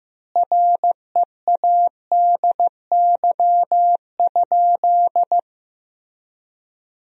Morse code, heard as a keyed tone, about 15 wpm, 700 Hz. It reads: READY?
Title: 3x Word-Spacing - Sentences from Top 400 Words - 15wpm